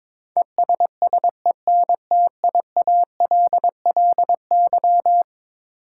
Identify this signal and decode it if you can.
Morse code, heard as a keyed tone, about 22 wpm, 700 Hz. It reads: ESSENTIALLY
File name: practice / 22wpm / Eleven Letter Words - 22wpm